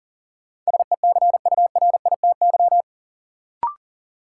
Morse code, heard as a keyed tone, about 40 wpm, 700 Hz. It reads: SECURITY
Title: Sentences from Top 1000 Words - Review Part 4 - New Word Drill - 40wpm